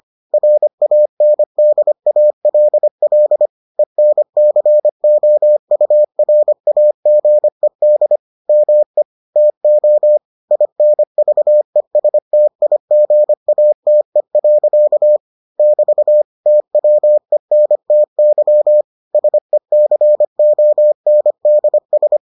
Morse code, heard as a keyed tone, about 25 wpm, 600 Hz. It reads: RANDALL ENCOURAGED ME TO INVESTIGATE. <BT> TWENTY SECONDS